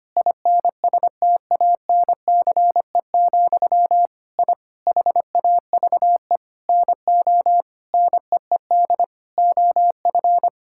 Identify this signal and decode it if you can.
Morse code, heard as a keyed tone, about 25 wpm, 700 Hz. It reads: INSTANCE, I HAVE NO NEED OF
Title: Sentences from Top 1000 Words - Review Part 5 - 25wpm